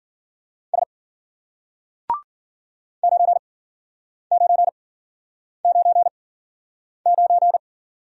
Morse code, heard as a keyed tone, about 53 wpm, spaced as for 9 wpm, 700 Hz. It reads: S 9999